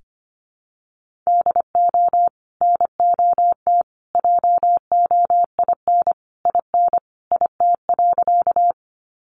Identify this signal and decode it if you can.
Morse code, heard as a keyed tone, about 25 wpm, 700 Hz. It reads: DO NOT JOIN IN IT.
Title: Sentences from Top 1400 Words 25wpm